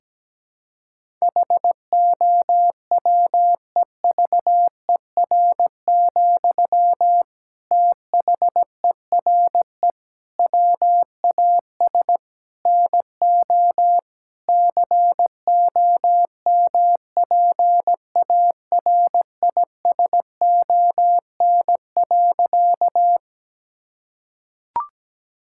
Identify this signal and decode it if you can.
Morse code, heard as a keyed tone, about 17 wpm, 700 Hz. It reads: HOWEVER, THERE WAS NO COMPARISON.
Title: Sentences from Top 1600 Words 17wpm